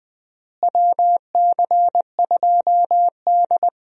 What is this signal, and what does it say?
Morse code, heard as a keyed tone, about 20 wpm, 700 Hz. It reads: WC2D
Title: USA Call Signs - 4 Characters - Repeated 3x - 20wpm